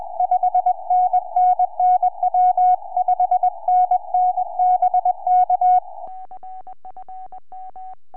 Morse code, heard as a keyed tone, about 21 wpm, 730 Hz. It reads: 5NN NW 5NN BK
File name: i5twk8 4m